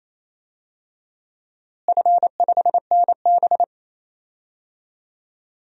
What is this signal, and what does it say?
Morse code, heard as a keyed tone, about 28 wpm, 700 Hz. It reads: F5NB
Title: DXCC-100 Call Signs - 4 Characters - Repeated 2x - 28wpm